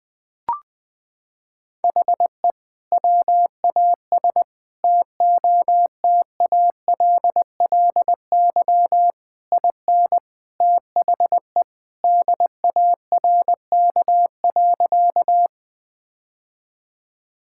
Morse code, heard as a keyed tone, about 21 wpm, 700 Hz. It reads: HE WAS TOTALLY IN THE DARK.
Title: Sentences from Top 1600 Words 20wpm